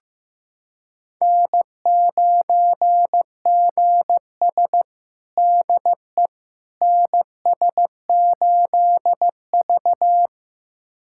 Morse code, heard as a keyed tone, about 15 wpm, 700 Hz. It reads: N9GS DE NS8V
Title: USA Callsign DE Callsign - 4 Characters - 15wpm